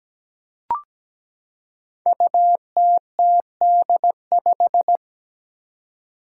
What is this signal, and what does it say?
Morse code, heard as a keyed tone, about 17 wpm, 700 Hz. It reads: UTTD5
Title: Five Challenging Letter-Number Combinations 17wpm